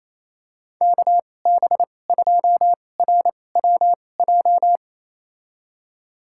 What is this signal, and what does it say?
Morse code, heard as a keyed tone, about 28 wpm, spaced as for 20 wpm, 700 Hz. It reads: KB2RWJ
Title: USA Call Signs - 6 Characters - 2x Character Spacing - 28wpm